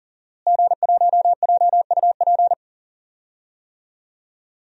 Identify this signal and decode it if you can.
Morse code, heard as a keyed tone, about 37 wpm, 700 Hz. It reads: G1JUP
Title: DXCC-100 Call Signs - 5 Characters - Repeated 3x - 40wpm